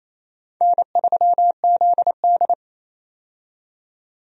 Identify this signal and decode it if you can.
Morse code, heard as a keyed tone, about 28 wpm, 700 Hz. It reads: N3ZD
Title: USA Call Signs - 4 Characters - 28wpm